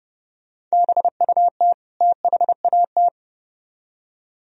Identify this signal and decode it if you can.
Morse code, heard as a keyed tone, about 34 wpm, 700 Hz. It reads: BUT THAT
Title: Speed-Racing - Sets of 2 Words 30wpm